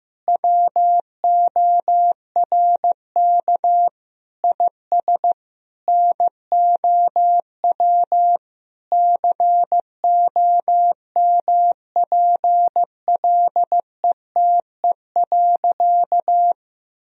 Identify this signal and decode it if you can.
Morse code, heard as a keyed tone, about 16 wpm, 700 Hz. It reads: WORK IS NOW COMPLETE.
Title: Sentences from Top 1000 Words - Review Part 2 - 15wpm